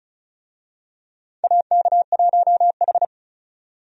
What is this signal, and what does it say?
Morse code, heard as a keyed tone, about 36 wpm, 700 Hz. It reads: AK1H
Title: USA Call Signs - 4 Characters - 35wpm